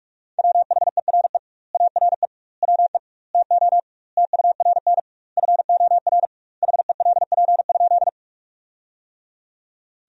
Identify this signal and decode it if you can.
Morse code, heard as a keyed tone, about 42 wpm, 700 Hz. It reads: WHERE ARE WE TO TURN FOR HELP?
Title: 2x Word-Spacing - Sentences from Top 200 Words - 45wpm